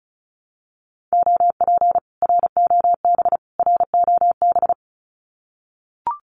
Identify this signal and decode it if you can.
Morse code, heard as a keyed tone, about 34 wpm, 700 Hz. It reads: OP ROB ROB
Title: Common QSO Phrases Encore 35wpm